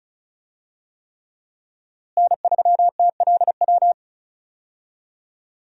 Morse code, heard as a keyed tone, about 36 wpm, 700 Hz. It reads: N3TLW
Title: USA Call Signs - 5 Characters - Repeated 3x - 35wpm